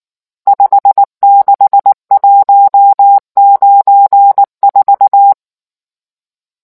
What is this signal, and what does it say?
Morse code, heard as a keyed tone, about 19 wpm, 800 Hz. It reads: 56194